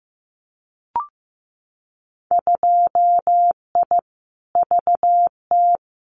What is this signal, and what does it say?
Morse code, heard as a keyed tone, about 15 wpm, 700 Hz. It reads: 2I VT
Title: Winter Field Day Exchanges 15wpm